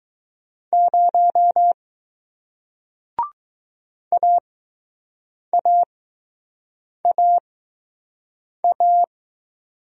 Morse code, heard as a keyed tone, about 20 wpm, spaced as for 6 wpm, 700 Hz. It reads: 0 AAAA